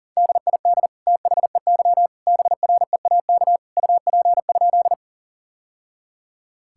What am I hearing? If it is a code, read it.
Morse code, heard as a keyed tone, about 40 wpm, 700 Hz. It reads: DID THEY BREAK UP?
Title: Sentences from Top 1000 Words - Review Part 6 - 40wpm